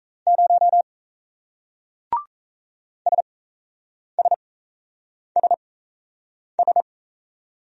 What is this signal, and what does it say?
Morse code, heard as a keyed tone, about 37 wpm, spaced as for 8 wpm, 700 Hz. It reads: 0 SSSS